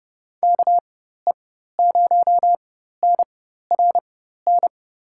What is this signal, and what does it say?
Morse code, heard as a keyed tone, about 32 wpm, spaced as for 14 wpm, 700 Hz. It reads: KE0NRN
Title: USA Call Signs - 4x Character Spacing - 30wpm